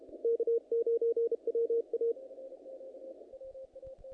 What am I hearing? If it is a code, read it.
Morse code, heard as a keyed tone, about 32 wpm, 460 Hz, with radio noise. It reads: K9WA
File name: contests / naqp-cw-k5tr-op-k5pi- -2000